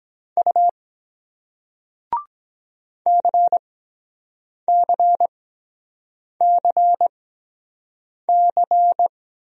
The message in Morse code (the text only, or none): U CCCC